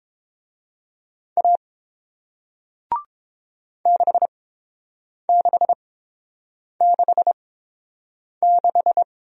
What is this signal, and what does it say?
Morse code, heard as a keyed tone, about 30 wpm, spaced as for 7 wpm, 700 Hz. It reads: A 6666